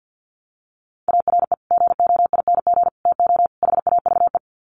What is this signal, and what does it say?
Morse code, heard as a keyed tone, about 50 wpm, 700 Hz. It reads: ARE GOING TO HAVE
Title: Sets of 4 Words 50wpm